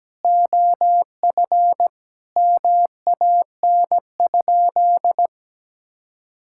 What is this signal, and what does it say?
Morse code, heard as a keyed tone, about 17 wpm, 700 Hz. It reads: OF MAN?
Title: Sentences from Top 1000 Words - Review Part 5 - 17wpm